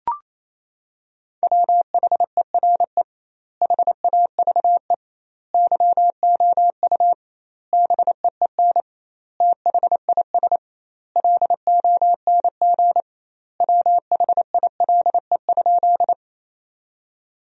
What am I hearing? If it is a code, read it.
Morse code, heard as a keyed tone, about 27 wpm, 700 Hz. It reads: WHERE HAVE YOU BEEN THIS LONG WHILE?
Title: 2x Word-Spacing - Sentences from Top 300 Words - 28wpm